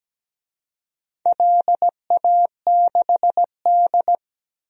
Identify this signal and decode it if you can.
Morse code, heard as a keyed tone, about 17 wpm, 700 Hz. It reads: LA6D
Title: DXCC-100 Call Signs - 4 Characters - Repeated 2x - 17wpm